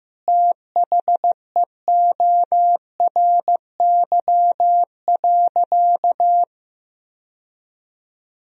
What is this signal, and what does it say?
Morse code, heard as a keyed tone, about 15 wpm, 700 Hz. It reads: THEORY.
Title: Sentences from Top 1000 Words - Review Part 3 - 15wpm